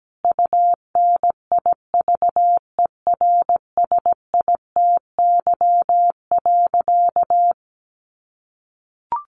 Morse code, heard as a keyed tone, about 17 wpm, 700 Hz. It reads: UNIVERSITY.
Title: Sentences from Top 1300 Words 17wpm